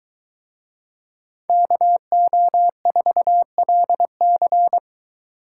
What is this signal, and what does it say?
Morse code, heard as a keyed tone, about 21 wpm, 700 Hz. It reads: KO4LC